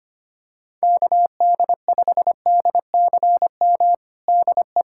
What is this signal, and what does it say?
Morse code, heard as a keyed tone, about 25 wpm, 700 Hz. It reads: KD5DCM DE
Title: USA Callsign DE Callsign 25wpm